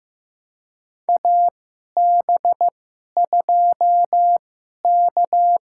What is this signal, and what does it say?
Morse code, heard as a keyed tone, about 15 wpm, spaced as for 11 wpm, 700 Hz. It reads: AB2K
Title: USA Call Signs - 4 Characters - 2x Character Spacing - 15wpm